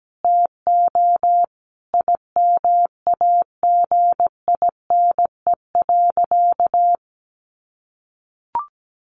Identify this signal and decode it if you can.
Morse code, heard as a keyed tone, about 17 wpm, 700 Hz. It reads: TO IMAGINE.
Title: Sentences from Top 1400 Words 17wpm